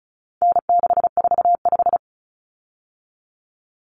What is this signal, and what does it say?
Morse code, heard as a keyed tone, about 34 wpm, 700 Hz. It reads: N645
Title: Common Cut Numbers - 4-Digit 35wpm